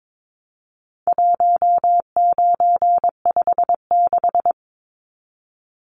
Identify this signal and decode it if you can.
Morse code, heard as a keyed tone, about 22 wpm, 700 Hz. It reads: 1956
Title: Numbers - 4-Digit 22wpm